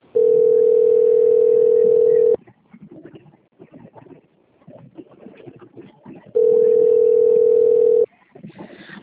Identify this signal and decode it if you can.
Ringback tone (North America), 440 + 480 Hz.